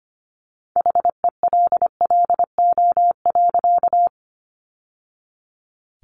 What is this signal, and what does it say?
Morse code, heard as a keyed tone, about 25 wpm, 700 Hz. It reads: HELLO.
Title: Sentences from Top 1100 Words 25wpm